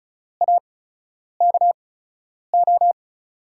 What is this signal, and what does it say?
Morse code, heard as a keyed tone, about 36 wpm, spaced as for 10 wpm, 700 Hz. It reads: AKO
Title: USA Call Signs - 8x Character Spacing - 35wpm